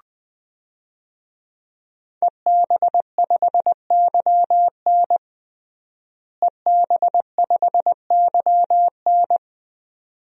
Morse code, heard as a keyed tone, about 21 wpm, 700 Hz. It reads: EB5YN EB5YN
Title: DXCC-100 Call Signs - 5 Characters - Repeated 2x - 20wpm